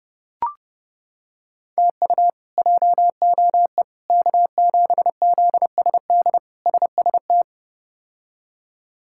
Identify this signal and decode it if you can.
Morse code, heard as a keyed tone, about 32 wpm, 700 Hz. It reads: TU JOE K7ZSD SST
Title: K1USN SST QSO Practice - 30wpm